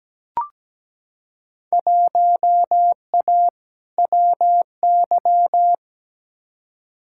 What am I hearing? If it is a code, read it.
Morse code, heard as a keyed tone, about 17 wpm, 700 Hz. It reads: 1A WY
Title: Field Day Exchanges Encore - 17wpm